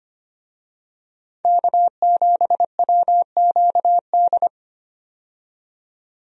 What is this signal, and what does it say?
Morse code, heard as a keyed tone, about 25 wpm, 700 Hz. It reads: K7WQD
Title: USA Call Signs - 5 Characters - 25wpm